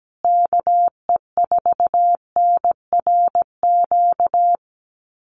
Morse code, heard as a keyed tone, about 17 wpm, 700 Hz. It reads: KE4NRQ
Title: USA Call Signs Repeated 2x 17wpm